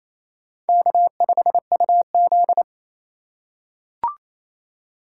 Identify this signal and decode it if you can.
Morse code, heard as a keyed tone, about 27 wpm, 690 Hz. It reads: K5UZ
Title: USA Call Signs - 4 Characters - 2x Character Spacing - 28wpm